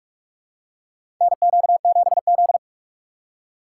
Morse code, heard as a keyed tone, about 47 wpm, 700 Hz. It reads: NQ7Z
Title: USA Call Signs - 4 Characters - 3x Character Spacing - 45wpm